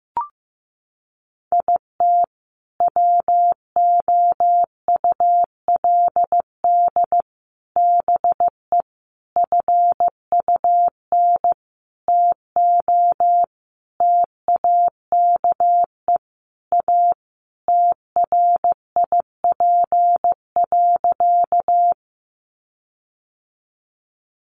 Morse code, heard as a keyed tone, about 15 wpm, 700 Hz. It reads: IT WOULD BE FUN TO TAKE A TRIP.
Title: Sentences from Top 1300 Words 15wpm